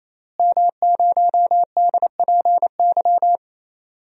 Morse code, heard as a keyed tone, about 27 wpm, 700 Hz. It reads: M0DPY
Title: DXCC-100 Call Signs - 5 Characters - Repeated 3x - 28wpm